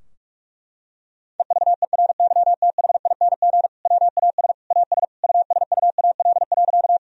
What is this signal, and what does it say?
Morse code, heard as a keyed tone, about 46 wpm, 700 Hz. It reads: EVERYTHING WAS AS USUAL.
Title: Sentences from Top 1000 Words - Review Part 2 - 45wpm